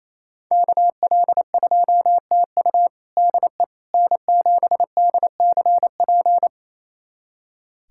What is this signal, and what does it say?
Morse code, heard as a keyed tone, about 28 wpm, 700 Hz. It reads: KL2TU DE N7DCP
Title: USA Callsign DE Callsign - 5 Characters - 28wpm